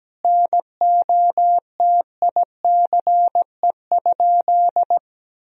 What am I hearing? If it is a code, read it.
Morse code, heard as a keyed tone, about 17 wpm, 700 Hz. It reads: NOTICE?
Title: Sentences from Top 1000 Words - Review Part 2 - 17wpm